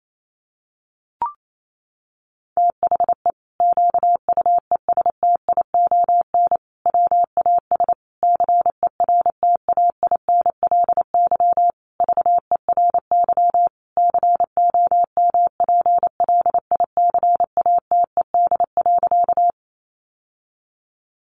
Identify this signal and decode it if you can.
Morse code, heard as a keyed tone, about 28 wpm, 700 Hz. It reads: THE QUESTION WAS CERTAINLY VERY COMPLICATED.
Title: Sentences from Top 1200 Words 28wpm